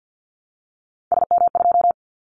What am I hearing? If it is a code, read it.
Morse code, heard as a keyed tone, about 50 wpm, 700 Hz. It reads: SM2
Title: Callsign Prefixes for Top 25 DXCC - Repeated 3x - 50wpm